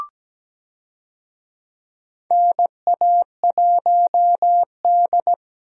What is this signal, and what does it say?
Morse code, heard as a keyed tone, about 17 wpm, 700 Hz. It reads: NA1D
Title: USA Call Signs - 4 Characters - Repeated 3x - 17wpm